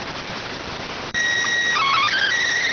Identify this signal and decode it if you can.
SSTV (slow-scan television), PD 290.